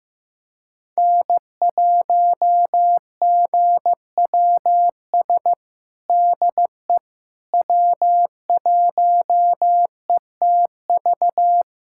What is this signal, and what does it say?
Morse code, heard as a keyed tone, about 15 wpm, 700 Hz. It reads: N1GWS DE W1ETV
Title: USA Callsign DE Callsign - 5 Characters - 15wpm